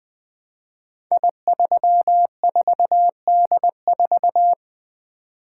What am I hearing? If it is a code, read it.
Morse code, heard as a keyed tone, about 20 wpm, 700 Hz. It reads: I34D4
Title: Code Groups - 5 Letter:Numbers - 20wpm